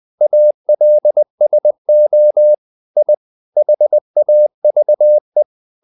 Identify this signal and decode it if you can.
Morse code, heard as a keyed tone, about 20 wpm, 600 Hz. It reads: ALSO I HAVE
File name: the old man and the sea part 2 20wpm